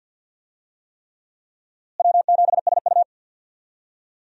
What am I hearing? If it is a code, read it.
Morse code, heard as a keyed tone, about 51 wpm, 700 Hz. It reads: W7SU
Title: USA Call Signs - 4 Characters - Repeated 3x - 50wpm